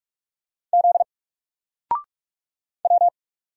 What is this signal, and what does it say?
Morse code, heard as a keyed tone, about 46 wpm, 700 Hz. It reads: Z W
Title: Single Letter-Number Alt 45wpm